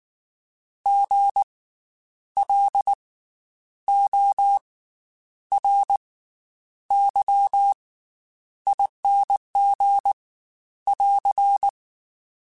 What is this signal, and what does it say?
Morse code, heard as a keyed tone, about 19 wpm, spaced as for 9 wpm, 800 Hz. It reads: GLORYING <AR>
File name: Special INGs / Special INGs-mp3s / Slow ing @18